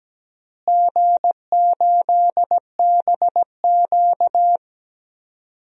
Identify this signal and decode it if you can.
Morse code, heard as a keyed tone, about 17 wpm, 700 Hz. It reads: G8BQ
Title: DXCC-100 Call Signs - 4 Characters - Repeated 3x - 17wpm